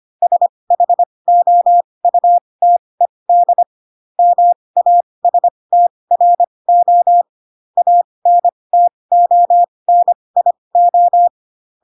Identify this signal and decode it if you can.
Morse code, heard as a keyed tone, about 25 wpm, spaced as for 20 wpm, 700 Hz. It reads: SHOUTED MASTRO ANTONIO
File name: Pinocchio / Pinocchio ch2 15 25 ik1hge